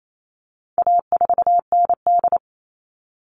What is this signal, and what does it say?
Morse code, heard as a keyed tone, about 28 wpm, 700 Hz. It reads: A4ND
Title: All Cut Numbers - 4-Digit 28wpm